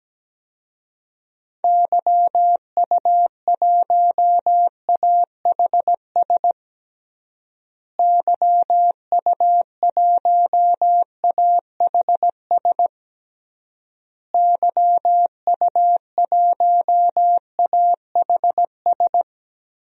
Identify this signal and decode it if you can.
Morse code, heard as a keyed tone, about 17 wpm, 700 Hz. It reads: YU1AHS YU1AHS YU1AHS
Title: DXCC-100 Call Signs - 6 Characters - Repeated 3x - 17wpm